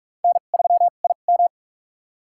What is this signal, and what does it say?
Morse code, heard as a keyed tone, about 43 wpm, spaced as for 32 wpm, 700 Hz. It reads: N3IM
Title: USA Call Signs - 4 Characters - 2x Character Spacing - 45wpm